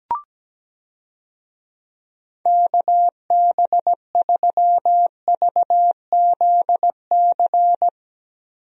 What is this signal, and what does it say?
Morse code, heard as a keyed tone, about 17 wpm, 700 Hz. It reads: KB3VZC
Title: USA Call Signs - 6 Characters - Repeated 3x - 17wpm